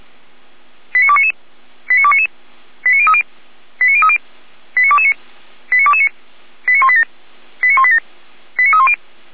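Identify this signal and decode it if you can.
Five-tone selective call (ZVEI), heard as quick runs of five tones, tones of about 70 ms. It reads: 89299 89299 89030 89030 89229 89229 89187 89187 89310
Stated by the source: ZVEI five-tone selective call received with an SDR by Feuerwehr Orlamünde.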